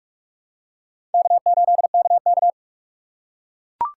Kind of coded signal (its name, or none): Morse code